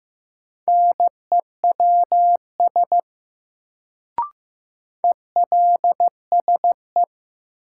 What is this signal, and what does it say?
Morse code, heard as a keyed tone, about 15 wpm, 700 Hz. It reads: NEWS ELSE
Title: Sentences from Top 800 Words - New Word Drill - 15wpm